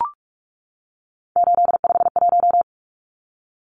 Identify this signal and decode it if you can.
Morse code, heard as a keyed tone, about 44 wpm, 700 Hz. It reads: 851
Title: Numbers - 3-Digit 45wpm